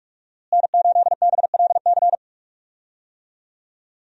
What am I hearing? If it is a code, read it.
Morse code, heard as a keyed tone, about 43 wpm, 700 Hz. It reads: N8BLC
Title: USA Call Signs - 5 Characters - Repeated 2x - 45wpm